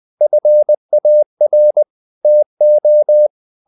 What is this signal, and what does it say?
Morse code, heard as a keyed tone, about 20 wpm, 600 Hz. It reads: FAR TO
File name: the old man and the sea / the old man and the sea part 1 20wpm